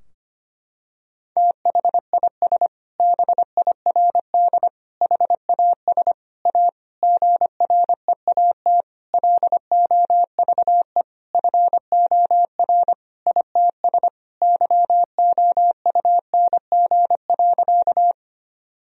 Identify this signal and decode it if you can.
Morse code, heard as a keyed tone, about 25 wpm, 700 Hz. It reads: THIS BIRD HAS A GREAT LOVE FOR ITS YOUNG.
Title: Sentences from Top 1000 Words - Review Part 2 - 25wpm